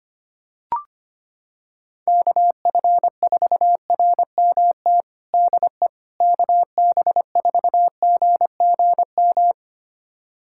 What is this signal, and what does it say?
Morse code, heard as a keyed tone, about 25 wpm, 700 Hz. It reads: KF4RMT DE KB4GGM
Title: USA Callsign DE Callsign - 6 Characters - 25wpm